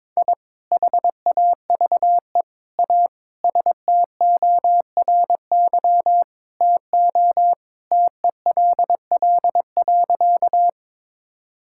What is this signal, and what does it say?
Morse code, heard as a keyed tone, about 22 wpm, 700 Hz. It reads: I HAVE A STORY TO TELL.
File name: practice / 22wpm / Sentences with 2-Word Phrases - 22wpm